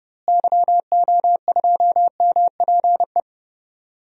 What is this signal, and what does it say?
Morse code, heard as a keyed tone, about 30 wpm, 700 Hz. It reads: YO2MPE
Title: DXCC-100 Call Signs - 6 Characters - Repeated 2x - 30wpm